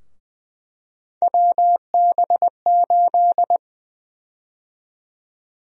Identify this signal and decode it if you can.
Morse code, heard as a keyed tone, about 20 wpm, 700 Hz. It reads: WB8
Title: USA Call Sign Prefixes - Repeated 2x - 20wpm